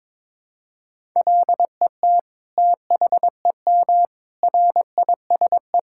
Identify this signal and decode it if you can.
Morse code, heard as a keyed tone, about 22 wpm, 700 Hz. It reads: LET THEM RISE
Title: Sentences from Top 1000 Words - Review Part 6 - 22wpm